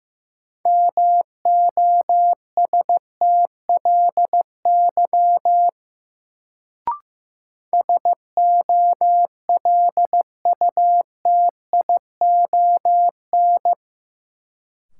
Morse code, heard as a keyed tone, about 15 wpm, 700 Hz. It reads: MOSTLY SOLUTION
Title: Sentences from Top 1300 Words - New Word Drill Alt - 15wpm